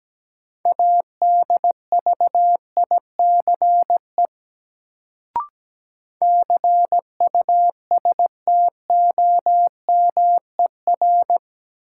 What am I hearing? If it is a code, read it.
Morse code, heard as a keyed tone, about 17 wpm, 700 Hz. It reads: ADVICE CUSTOMER
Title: Sentences from Top 1100 Words - New Word Drill - 17wpm